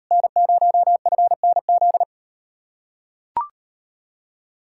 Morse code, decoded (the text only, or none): N0FNZ